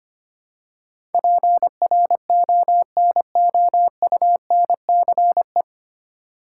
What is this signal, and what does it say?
Morse code, heard as a keyed tone, about 25 wpm, 700 Hz. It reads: PRONOUNCE
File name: practice / 25wpm / 9 Letter Words - First 3 Letters Given - 25wpm